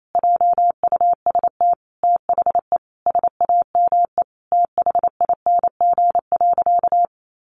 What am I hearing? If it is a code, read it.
Morse code, heard as a keyed tone, about 28 wpm, 700 Hz. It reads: JUST THE SAME THING.
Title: Sentences from Top 800 Words 28wpm